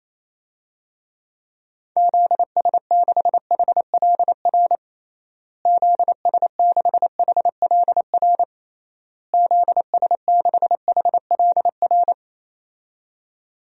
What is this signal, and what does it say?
Morse code, heard as a keyed tone, about 28 wpm, 700 Hz. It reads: ZS6HLR ZS6HLR ZS6HLR
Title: DXCC-100 Call Signs - 6 Characters - Repeated 3x - 28wpm